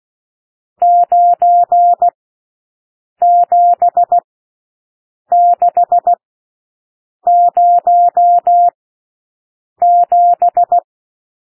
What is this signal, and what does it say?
Morse code, heard as a keyed tone, about 16 wpm, spaced as for 6 wpm, 700 Hz. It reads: 97607
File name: SOUND NUMBERS 16 wpm